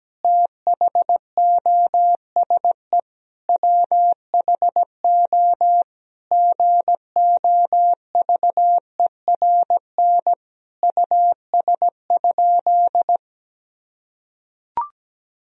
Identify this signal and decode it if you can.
Morse code, heard as a keyed tone, about 17 wpm, 700 Hz. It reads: THOSE WHO GOVERN US?
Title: Sentences from Top 1000 Words - Review Part 2 - 17wpm